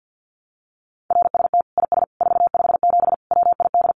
Morse code, heard as a keyed tone, about 50 wpm, 700 Hz. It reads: RST IS 457 WID
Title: Common QSO Phrases 50wpm